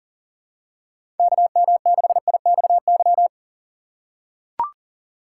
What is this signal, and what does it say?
Morse code, heard as a keyed tone, about 40 wpm, 700 Hz. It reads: KM6IXY
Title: USA Call Signs - 6 Characters - 4x Character Spacing - 40wpm